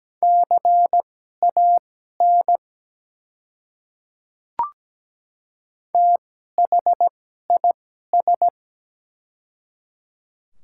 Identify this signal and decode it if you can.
Morse code, heard as a keyed tone, about 17 wpm, spaced as for 12 wpm, 700 Hz. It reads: CAN THIS
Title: 2x Character Spacing - Top 100 Words - 17wpm